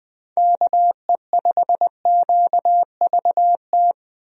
Morse code, heard as a keyed tone, about 20 wpm, 700 Hz. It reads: KE5QVT
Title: USA Call Signs - 6 Characters - 20wpm